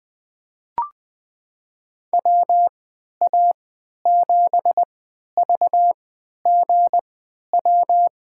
Morse code, heard as a keyed tone, about 20 wpm, spaced as for 11 wpm, 700 Hz. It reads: WA7VGW
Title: USA Call Signs - 6 Characters - 3x Character Spacing - 20wpm